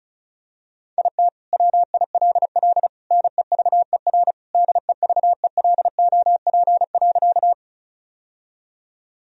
Morse code, heard as a keyed tone, about 36 wpm, 700 Hz. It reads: IT WILL NEVER DEVELOP.
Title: Sentences from Top 1000 Words - Review Part 2 - 35wpm